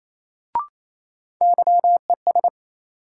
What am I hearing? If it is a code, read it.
Morse code, heard as a keyed tone, about 28 wpm, 700 Hz. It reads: YES
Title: Sentences from Top 500 Words - New Word Drill - 28wpm